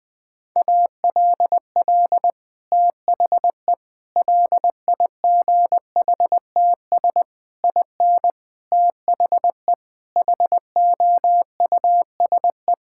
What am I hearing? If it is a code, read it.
Morse code, heard as a keyed tone, about 20 wpm, 700 Hz. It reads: ALL THE LIGHTS IN THE HOUSE
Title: Sentences with 2-Word Phrases - Spoken Beginning - 20wpm